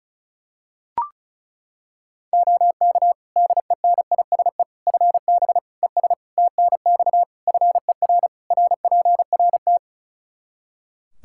Morse code, heard as a keyed tone, about 36 wpm, 700 Hz. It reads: OK DENISE FB ES TNX FER RPRT
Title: Practice QSO - Exchange 2 - Info - 35wpm